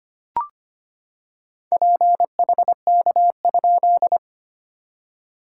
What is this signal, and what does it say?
Morse code, heard as a keyed tone, about 25 wpm, 660 Hz. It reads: PHK?
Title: SOTA QSO Practice 25wpm